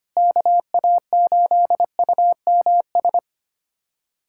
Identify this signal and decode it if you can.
Morse code, heard as a keyed tone, about 25 wpm, 700 Hz. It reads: KA8UMS
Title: USA Callsign DE Callsign - 6 Characters - 25wpm